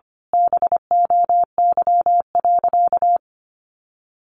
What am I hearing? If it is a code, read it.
Morse code, heard as a keyed tone, about 25 wpm, 700 Hz. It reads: BOY.
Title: Sentences from Top 800 Words 25wpm